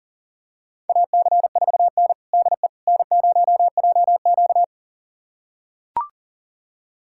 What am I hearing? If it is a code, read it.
Morse code, heard as a keyed tone, about 40 wpm, 700 Hz. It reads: AC4N DE N0JQ
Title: USA Callsign DE Callsign - 4 Characters - 40wpm